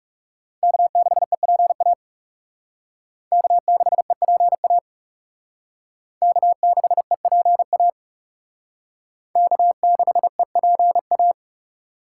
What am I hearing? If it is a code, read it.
Morse code, heard as a keyed tone, about 37 wpm, 700 Hz. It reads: K6EPA K6EPA K6EPA K6EPA
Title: Speed Racing - USA 1x3 Call Signs 30wpm